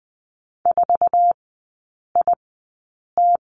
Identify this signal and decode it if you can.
Morse code, heard as a keyed tone, about 20 wpm, spaced as for 8 wpm, 700 Hz. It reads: 4IT
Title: 047 Lesson 9e - Number 4 - Three Characters 20wpm